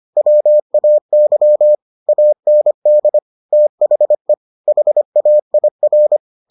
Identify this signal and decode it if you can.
Morse code, heard as a keyed tone, about 25 wpm, 600 Hz. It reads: WAY AND THE HAIR